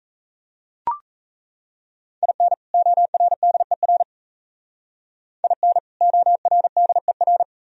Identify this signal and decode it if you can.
Morse code, heard as a keyed tone, about 33 wpm, 700 Hz. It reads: IN ORDER IN ORDER
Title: Speed-Racing - Sets of 2 Words 28wpm